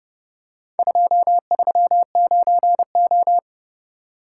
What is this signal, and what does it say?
Morse code, heard as a keyed tone, about 26 wpm, 700 Hz. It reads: 239O